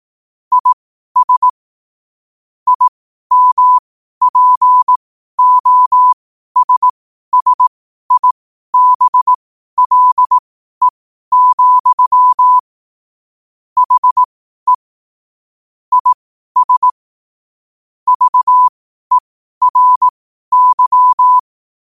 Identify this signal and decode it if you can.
Morse code, heard as a keyed tone, about 18 wpm, spaced as for 12 wpm, 1000 Hz. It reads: IS IMPOSSIBLE, HE IS VERY
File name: qotdfiles / 12wpm